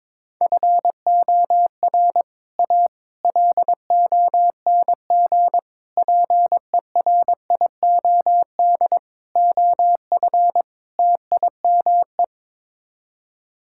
Morse code, heard as a keyed tone, about 22 wpm, 700 Hz. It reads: FOR A LONG PERIOD OF TIME
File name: practice / 22wpm / Sets of 6 Words - 22wpm